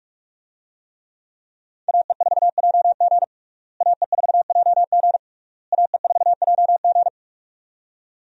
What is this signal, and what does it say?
Morse code, heard as a keyed tone, about 45 wpm, 700 Hz. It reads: AE4JG AE4JG AE4JG
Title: USA Call Signs - 5 Characters - Repeated 3x - 45wpm